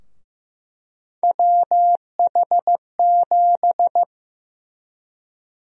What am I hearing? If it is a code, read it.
Morse code, heard as a keyed tone, about 16 wpm, 700 Hz. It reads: WH7
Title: USA Call Sign Prefixes - Repeated 2x - 15wpm